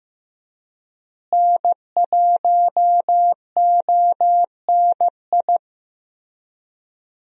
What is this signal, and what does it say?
Morse code, heard as a keyed tone, about 14 wpm, 700 Hz. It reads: N1ONI
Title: USA Call Signs - 5 Characters - Repeated 3x - 15wpm